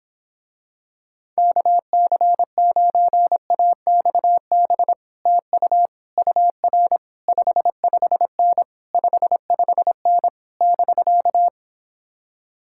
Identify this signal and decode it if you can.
Morse code, heard as a keyed tone, about 26 wpm, 700 Hz. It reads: KC9AXB TU UR 55N 55N <BK>